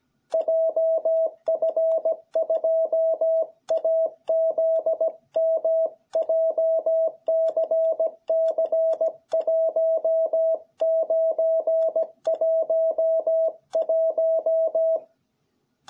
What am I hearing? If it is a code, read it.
Morse code, heard as a keyed tone, about 17 wpm, 660 Hz. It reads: JF2AZM JCC1911